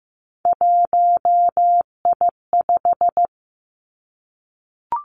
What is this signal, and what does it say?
Morse code, heard as a keyed tone, about 15 wpm, 700 Hz. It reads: 1I5
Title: Three Challenging Letter-Number Combinations 15wpm